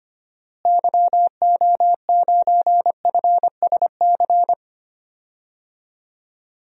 Morse code, heard as a keyed tone, about 25 wpm, 700 Hz. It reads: YO9FSC
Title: DXCC-100 Call Signs - 6 Characters - Repeated 2x - 25wpm